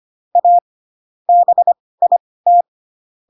Morse code, heard as a keyed tone, about 25 wpm, spaced as for 18 wpm, 700 Hz. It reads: A BIT